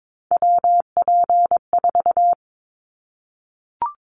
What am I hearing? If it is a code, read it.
Morse code, heard as a keyed tone, about 22 wpm, 700 Hz. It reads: WP4
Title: Callsign Prefixes for Top 25 DXCC - 22wpm